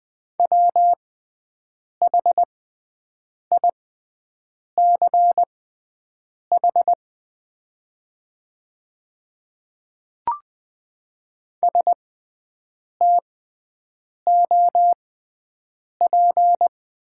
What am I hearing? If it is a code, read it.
Morse code, heard as a keyed tone, about 20 wpm, spaced as for 7 wpm, 700 Hz. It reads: WHICH STOP
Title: 6x Character Spacing - Top 300 Words - 20wpm